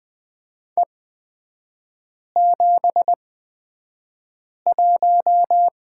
Morse code, heard as a keyed tone, about 20 wpm, spaced as for 5 wpm, 700 Hz. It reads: E71
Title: 224 Lesson 36e - BT - Three Characters 20wpm